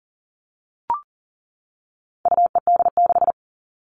Morse code, heard as a keyed tone, about 40 wpm, 700 Hz. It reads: UED6
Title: All Cut Numbers - 4-Digit 40wpm